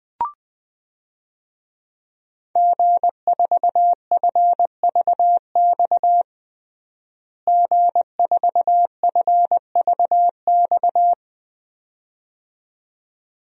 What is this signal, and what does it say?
Morse code, heard as a keyed tone, about 21 wpm, 700 Hz. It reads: G4FVX G4FVX
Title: DXCC-100 Call Signs - 5 Characters - Repeated 2x - 20wpm